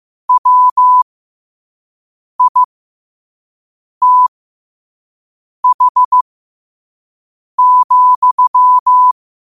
Morse code, heard as a keyed tone, about 15 wpm, spaced as for 5 wpm, 1000 Hz. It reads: WITH,
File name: qotdfiles / 5wpm